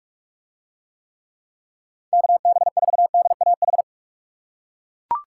Morse code, heard as a keyed tone, about 40 wpm, 700 Hz. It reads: KB4DAH